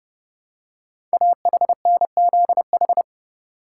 Morse code, heard as a keyed tone, about 30 wpm, 700 Hz. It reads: AHNZH
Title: Code Groups - 5 Letters - 30wpm